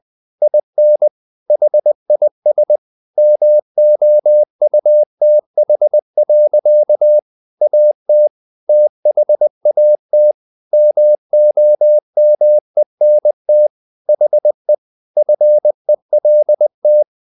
Morse code, heard as a keyed tone, about 20 wpm, 600 Hz. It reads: IN HIS MOUTH. AT THAT MOMENT HE FELT
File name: the old man and the sea / the old man and the sea part 1 20wpm